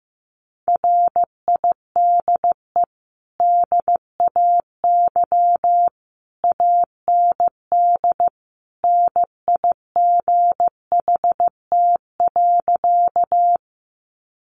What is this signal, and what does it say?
Morse code, heard as a keyed tone, about 15 wpm, 700 Hz. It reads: RIDE DAY AND NIGHT.
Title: Sentences from Top 1200 Words 15wpm